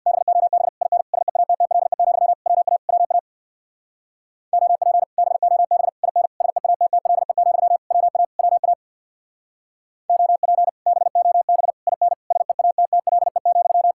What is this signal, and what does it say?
Morse code, heard as a keyed tone, about 60 wpm, 700 Hz. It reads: BOB IN SEATTLE, WA WA OP BOB IN SEATTLE, WA WA OP BOB IN SEATTLE,
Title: Speed-Racing Common QSO Phrases - 50wpm